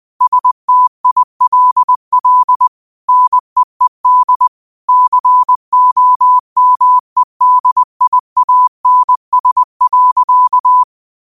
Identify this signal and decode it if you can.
Morse code, heard as a keyed tone, about 20 wpm, 1000 Hz. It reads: STILL NEED COMEDIANS.
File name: qotdfiles / 20wpm